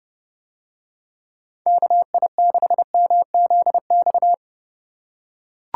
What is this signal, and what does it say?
Morse code, heard as a keyed tone, about 30 wpm, 700 Hz. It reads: KI6MZX